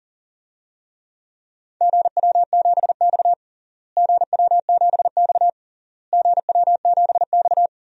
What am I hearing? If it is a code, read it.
Morse code, heard as a keyed tone, about 40 wpm, 700 Hz. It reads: GW7X GW7X GW7X
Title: DXCC-100 Call Signs - 4 Characters - Repeated 3x - 40wpm